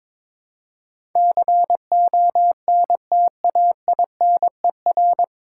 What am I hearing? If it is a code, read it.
Morse code, heard as a keyed tone, about 23 wpm, 700 Hz. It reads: CONTAINER
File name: 9 Letter Words - First 3 Letters Given - 22wpm